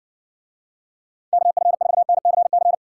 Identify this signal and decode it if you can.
Morse code, heard as a keyed tone, about 60 wpm, 700 Hz. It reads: KF4NCC